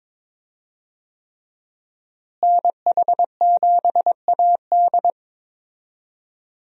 Morse code, heard as a keyed tone, about 22 wpm, 700 Hz. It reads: NH7AD
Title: USA Call Signs - 5 Characters - Repeated 3x - 22wpm